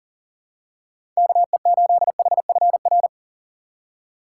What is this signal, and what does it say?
Morse code, heard as a keyed tone, about 40 wpm, 700 Hz. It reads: KE8HFR
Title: USA Call Signs - 6 Characters - 2x Character Spacing - 40wpm